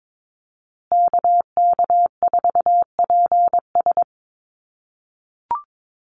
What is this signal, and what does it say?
Morse code, heard as a keyed tone, about 22 wpm, 700 Hz. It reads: KK4PS
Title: USA Call Signs Repeated 3x 22wpm